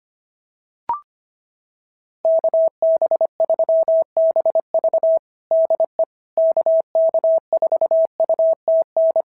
Morse code, heard as a keyed tone, about 25 wpm, 690 Hz. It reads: KB3BV DE KK4UTN
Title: SOTA QSO Practice 25wpm